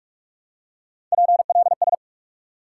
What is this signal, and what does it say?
Morse code, heard as a keyed tone, about 46 wpm, 700 Hz. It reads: PLS
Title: Top 90 QSO Elements 45wpm